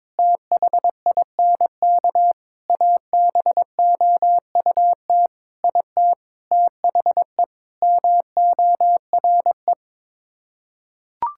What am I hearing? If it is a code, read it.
Morse code, heard as a keyed tone, about 22 wpm, 700 Hz. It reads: THINK ABOUT IT THE MORE
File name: practice / 22wpm / Sets of 8 Words - 22wpm